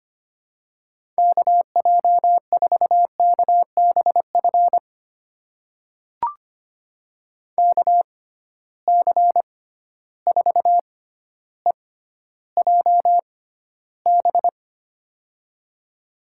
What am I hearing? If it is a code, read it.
Morse code, heard as a keyed tone, about 25 wpm, spaced as for 13 wpm, 700 Hz. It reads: KJ4KBF KC4EJB
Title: USA Call Signs - 6 Characters - 6x Character Spacing - 25wpm